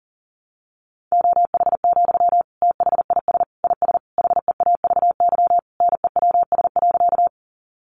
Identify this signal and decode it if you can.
Morse code, heard as a keyed tone, about 40 wpm, 700 Hz. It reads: OH, THIS IS HEAVY NEWS.
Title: Sentences from Top 1100 Words 40wpm